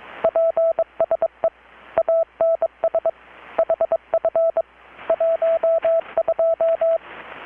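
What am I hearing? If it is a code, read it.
Morse code, heard as a keyed tone, about 22 wpm, 660 Hz, with radio noise. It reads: PSE ANS HF 12
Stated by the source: Morse from KPH's Night of Nights XX maritime radiotelegraphy event, received in Chiapas, Mexico about 3,800 km away on a Kenwood R-600 with a grounded 10 m long-wire antenna.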